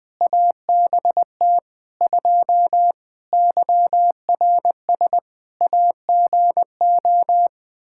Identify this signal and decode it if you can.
Morse code, heard as a keyed tone, about 20 wpm, 700 Hz. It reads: ABT 2 YRS AGO
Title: K1USN SST QSO Practice 20wpm